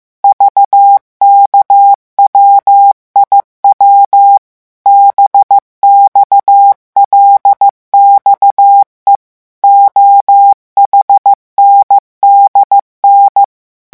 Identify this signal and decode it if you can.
Morse code, heard as a keyed tone, about 15 wpm, 800 Hz. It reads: VKWIW BXLXE OHNDN
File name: morse mp3 / revision8-vitesse 14 mots